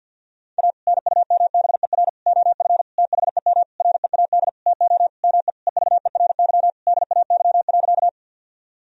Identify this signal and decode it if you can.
Morse code, heard as a keyed tone, about 52 wpm, 700 Hz. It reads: A NUMBER OF THEM READ TO ME EVERY DAY.
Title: Sentences from Top 1000 Words - Review Part 1 - 50wpm